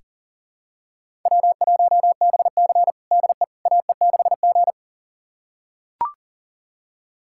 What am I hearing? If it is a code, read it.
Morse code, heard as a keyed tone, about 40 wpm, 700 Hz. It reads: W1BC DE AE6G
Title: USA Callsign DE Callsign - 4 Characters - 40wpm